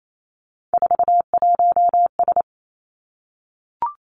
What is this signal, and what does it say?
Morse code, heard as a keyed tone, about 28 wpm, 700 Hz. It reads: 41S
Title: Three Challenging Letter-Number Combinations 28wpm